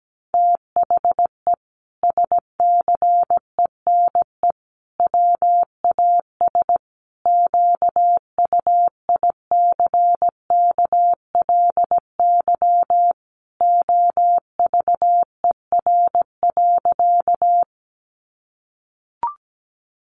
Morse code, heard as a keyed tone, about 17 wpm, 700 Hz. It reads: THE SCENE WAS QUICKLY OVER.
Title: Sentences from Top 1100 Words 17wpm